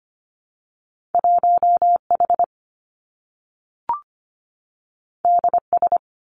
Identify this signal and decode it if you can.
Morse code, heard as a keyed tone, about 25 wpm, 700 Hz. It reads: 1H DS
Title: Two Challenging Letter-Number Combinations 25wpm